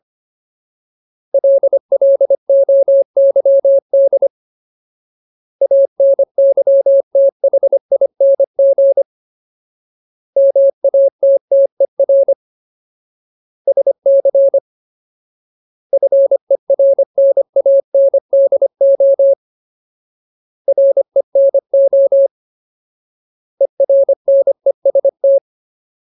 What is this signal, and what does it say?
Morse code, heard as a keyed tone, about 25 wpm, 550 Hz. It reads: LLOYD ANYTHING MATTER SC FERNANDO RENO ERNEST
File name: Ultimate Word List / 25WPM Ultimate